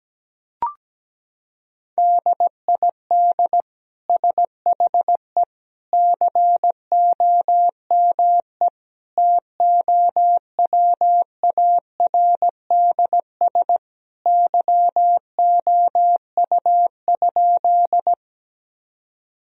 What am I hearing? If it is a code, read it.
Morse code, heard as a keyed tone, about 17 wpm, 700 Hz. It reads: DID SHE COME TOWARDS YOU?
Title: Sentences from Top 1000 Words - Review Part 2 - 17wpm